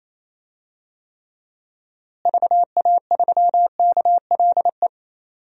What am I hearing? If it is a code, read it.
Morse code, heard as a keyed tone, about 28 wpm, 700 Hz. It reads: VA3KLE
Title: DXCC-100 Call Signs - 6 Characters - Repeated 3x - 28wpm